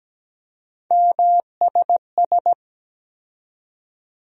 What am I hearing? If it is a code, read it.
Morse code, heard as a keyed tone, about 17 wpm, 700 Hz. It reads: MSS
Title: USA Call Sign Postfixes - 3 Letters 17wpm